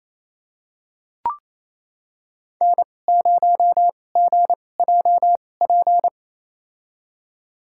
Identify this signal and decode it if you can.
Morse code, heard as a keyed tone, about 28 wpm, spaced as for 20 wpm, 700 Hz. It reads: N0GJP
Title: USA Call Signs - 2x Character Spacing - 28wpm